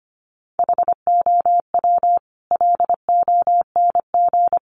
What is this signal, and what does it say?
Morse code, heard as a keyed tone, about 25 wpm, 700 Hz. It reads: HOW LONG
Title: Sentences from Top 500 Words 25wpm